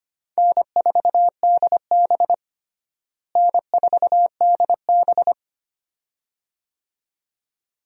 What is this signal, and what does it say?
Morse code, heard as a keyed tone, about 25 wpm, 700 Hz. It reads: N4DB N4DB
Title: USA Call Signs - 4 Characters - Repeated 2x - 25wpm